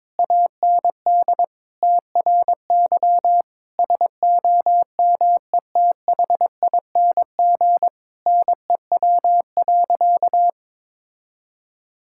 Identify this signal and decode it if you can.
Morse code, heard as a keyed tone, about 21 wpm, 700 Hz. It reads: AND TRY SOMETHING NEW.
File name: practice / 22wpm / Sentences with 8-Word Phrases - 22wpm